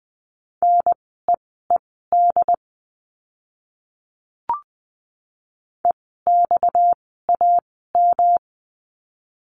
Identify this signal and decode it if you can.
Morse code, heard as a keyed tone, about 20 wpm, spaced as for 14 wpm, 700 Hz. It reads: NEED EXAM
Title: 230 Lesson 37f - Letter x - Words 20wpm